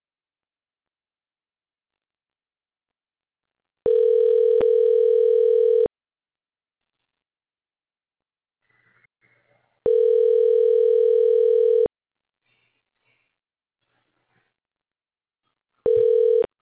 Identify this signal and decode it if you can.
Ringback tone (North America), 440 + 480 Hz.
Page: the number you have dialed is not in service please check the number and try again